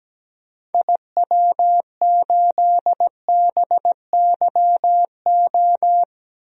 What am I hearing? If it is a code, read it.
Morse code, heard as a keyed tone, about 17 wpm, 700 Hz. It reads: IW8BYO
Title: DXCC-100 Call Signs - 6 Characters - 17wpm